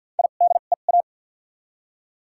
Morse code, heard as a keyed tone, about 53 wpm, spaced as for 37 wpm, 700 Hz. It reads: IDEA
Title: 2x Character Spacing - Top 300 Words - 50wpm